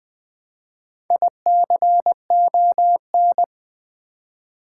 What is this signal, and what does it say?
Morse code, heard as a keyed tone, about 20 wpm, 700 Hz. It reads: ICON
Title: Build A Word - 1 Letter At A Time - 4-Letter Words - 20wpm